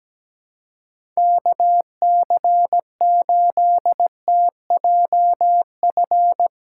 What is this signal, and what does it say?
Morse code, heard as a keyed tone, about 17 wpm, 700 Hz. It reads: KC8TJF
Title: USA Call Signs - 6 Characters - 2x Character Spacing - 17wpm